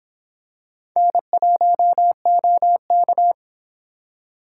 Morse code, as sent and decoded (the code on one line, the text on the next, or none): -. .---- --- -.-
N1OK